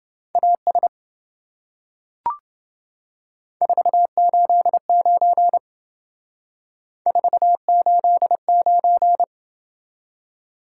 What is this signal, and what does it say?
Morse code, heard as a keyed tone, about 28 wpm, 700 Hz. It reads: AS 489 489